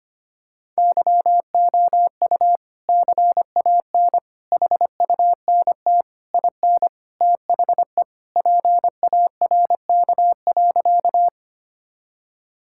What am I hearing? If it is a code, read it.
Morse code, heard as a keyed tone, about 25 wpm, 700 Hz. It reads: YOU CAN HUNT IN THE PARK.
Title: Sentences from Top 1600 Words 25wpm